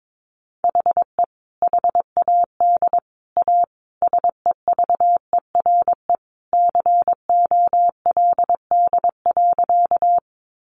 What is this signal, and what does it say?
Morse code, heard as a keyed tone, about 22 wpm, 700 Hz. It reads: HE HAD A SEVERE COLD.
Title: Sentences from Top 1400 Words 22wpm